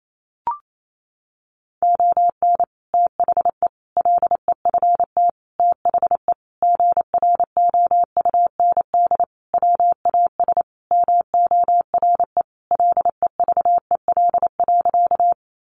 Morse code, heard as a keyed tone, about 28 wpm, 700 Hz. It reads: ON THE LEFT THE GROUND WAS MORE LEVEL.
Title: Sentences from Top 700 Words 28wpm